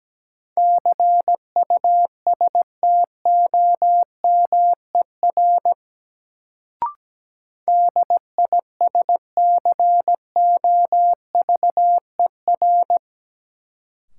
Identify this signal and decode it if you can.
Morse code, heard as a keyed tone, about 17 wpm, 700 Hz. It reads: CUSTOMER DISCOVER
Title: Sentences from Top 1100 Words - New Word Drill Alt - 17wpm